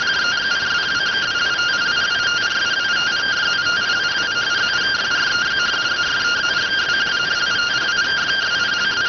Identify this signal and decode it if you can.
Two-tone teleprinter signal, about 170 Hz shift.